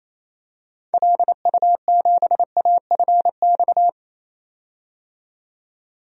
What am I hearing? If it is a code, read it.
Morse code, heard as a keyed tone, about 28 wpm, 700 Hz. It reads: LU7AFX
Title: DXCC-100 Call Signs - 6 Characters - 28wpm